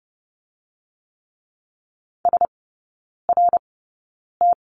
Morse code, heard as a keyed tone, about 30 wpm, spaced as for 9 wpm, 700 Hz. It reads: SRT
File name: Lesson 18e - Letter m - Three Characters 30wpm